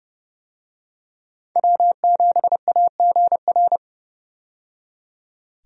Morse code, heard as a keyed tone, about 30 wpm, 700 Hz. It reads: W7AGR
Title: USA Call Signs - 5 Characters - 30wpm